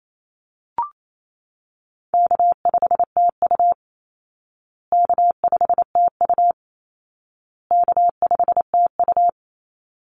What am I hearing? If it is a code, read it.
Morse code, heard as a keyed tone, about 28 wpm, 700 Hz. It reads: K5TU K5TU K5TU
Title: Call Signs Repeated 2x - 28wpm